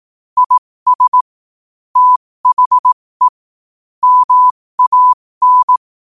Morse code, heard as a keyed tone, about 18 wpm, spaced as for 15 wpm, 1000 Hz. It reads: IS THE MAN
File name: qotdfiles / 15wpm